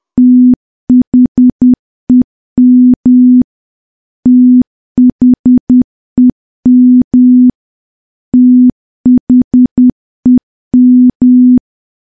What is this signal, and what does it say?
Morse code, heard as a keyed tone, about 10 wpm, 250 Hz. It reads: THEM THEM THEM
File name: morse / morse-training-10WPM